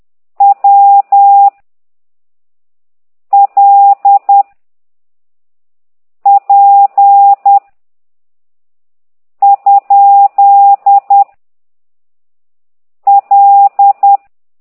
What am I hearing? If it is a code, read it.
Morse code, heard as a keyed tone, about 10 wpm, 800 Hz. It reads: W L P ? L
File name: ct2idw / exerc8